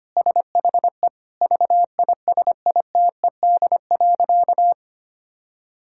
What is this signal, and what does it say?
Morse code, heard as a keyed tone, about 25 wpm, 700 Hz. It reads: SHE VISITED.
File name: practice / 25wpm / Sentences with 2-Word Phrases - 25wpm